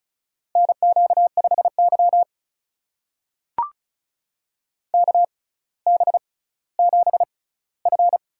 Morse code, heard as a keyed tone, about 36 wpm, spaced as for 18 wpm, 700 Hz. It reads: NQ5Y KB7F
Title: USA Call Signs - 4 Characters - 6x Character Spacing - 35wpm